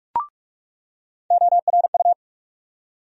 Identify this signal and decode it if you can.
Morse code, heard as a keyed tone, about 47 wpm, 700 Hz. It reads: ORU
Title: USA Call Sign Postfixes - 3 Letters 45wpm